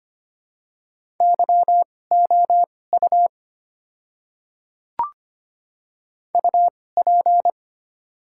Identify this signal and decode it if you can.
Morse code, heard as a keyed tone, about 25 wpm, spaced as for 18 wpm, 700 Hz. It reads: YOU UP
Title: 2x Character Spacing - Top 300 Words - 25wpm